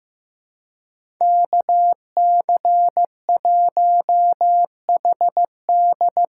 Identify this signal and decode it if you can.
Morse code, heard as a keyed tone, about 15 wpm, 700 Hz. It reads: KC1HD
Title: USA Call Signs - 8x Character Spacing - 15wpm